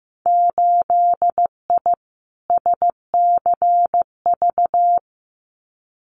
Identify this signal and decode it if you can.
Morse code, heard as a keyed tone, about 15 wpm, 700 Hz. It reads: 8I SCV
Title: Winter Field Day Exchanges Encore 15wpm